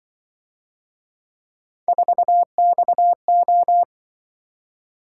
Morse code, heard as a keyed tone, about 25 wpm, 700 Hz. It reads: 4XO
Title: Speed-Racing - Three Letter-Number Combinations - 20wpm